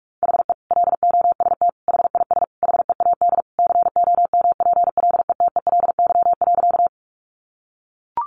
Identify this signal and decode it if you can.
Morse code, heard as a keyed tone, about 44 wpm, 700 Hz. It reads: HE LOST HIS HEAD COMPLETELY.
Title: Sentences from Top 1400 Words 45wpm